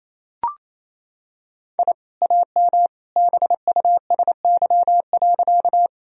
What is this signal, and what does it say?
Morse code, heard as a keyed tone, about 28 wpm, 690 Hz. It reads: I AM BUSY.
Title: Sentences from Top 1000 Words - Review Part 3 - 28wpm